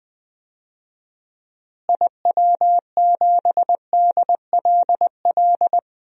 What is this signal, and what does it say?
Morse code, heard as a keyed tone, about 20 wpm, 700 Hz. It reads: IW7DLL
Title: DXCC-100 Call Signs - 6 Characters - Repeated 3x - 20wpm